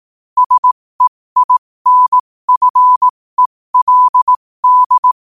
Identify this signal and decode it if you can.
Morse code, heard as a keyed tone, about 18 wpm, spaced as for 15 wpm, 1000 Hz. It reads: SEINFELD
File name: qotdfiles / 15wpm